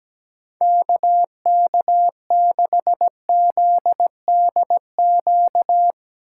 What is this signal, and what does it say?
Morse code, heard as a keyed tone, about 17 wpm, 700 Hz. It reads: KK6ZDQ
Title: USA Call Signs - 6 Characters - 8x Character Spacing - 17wpm